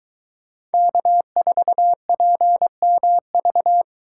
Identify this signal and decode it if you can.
Morse code, heard as a keyed tone, about 22 wpm, 700 Hz. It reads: K4PMV